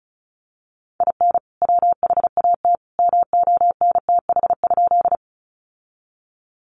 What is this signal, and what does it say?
Morse code, heard as a keyed tone, about 34 wpm, 700 Hz. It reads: IN WHAT MONTH?
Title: Sentences from Top 800 Words 35wpm